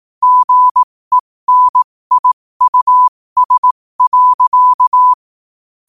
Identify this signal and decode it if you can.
Morse code, heard as a keyed tone, about 18 wpm, spaced as for 15 wpm, 1000 Hz. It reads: GENIUS.
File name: qotdfiles / 15wpm